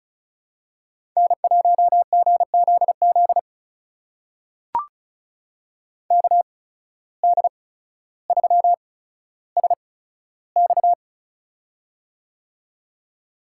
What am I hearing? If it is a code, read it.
Morse code, heard as a keyed tone, about 36 wpm, spaced as for 15 wpm, 700 Hz. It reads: N1GZZ KD3SX
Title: USA Call Signs - 5 Characters - 8x Character Spacing - 35wpm